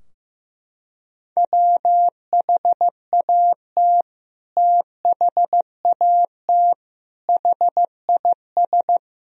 Morse code, heard as a keyed tone, about 15 wpm, 700 Hz. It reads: WHAT THAT HIS
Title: Sentences from Top 1000 Words - Review Part 2 - 15wpm